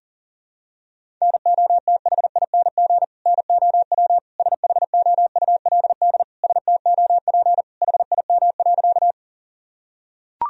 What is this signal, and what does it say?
Morse code, heard as a keyed tone, about 40 wpm, 700 Hz. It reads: NOTHING NOW SHOULD STOP HIM.
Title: Sentences from Top 1000 Words - Review Part 1 - 40wpm